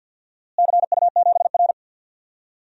Morse code, heard as a keyed tone, about 52 wpm, 700 Hz. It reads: CU7R
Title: DXCC-100 Call Signs - 4 Characters - Repeated 2x - 50wpm